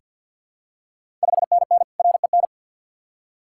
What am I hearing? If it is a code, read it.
Morse code, heard as a keyed tone, about 50 wpm, 700 Hz. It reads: 5NN REN
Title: Ontario QSO Party Exchange 50wpm